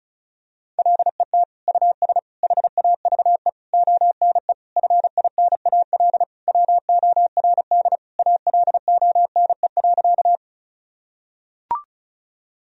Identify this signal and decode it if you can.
Morse code, heard as a keyed tone, about 36 wpm, 700 Hz. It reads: LET US HAVE ONE FINAL WORD ALONE.
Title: Sentences from Top 1000 Words - Review Part 2 - 35wpm